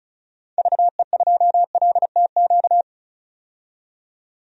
Morse code, heard as a keyed tone, about 36 wpm, 700 Hz. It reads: VE2LTQ
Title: DXCC-100 Call Signs - 6 Characters - 35wpm